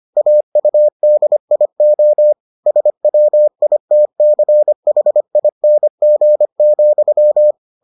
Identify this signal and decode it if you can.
Morse code, heard as a keyed tone, about 25 wpm, 600 Hz. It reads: AUDIO SWITCHING,